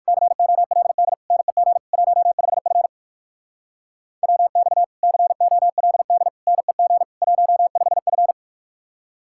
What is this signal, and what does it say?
Morse code, heard as a keyed tone, about 62 wpm, 700 Hz. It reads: COLD NEG 15F WX COLD NEG 15F